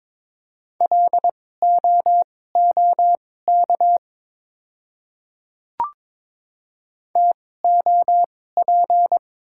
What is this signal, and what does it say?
Morse code, heard as a keyed tone, about 22 wpm, spaced as for 16 wpm, 690 Hz. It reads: LOOK TOP
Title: 2x Character Spacing - Top 500 Words - 22wpm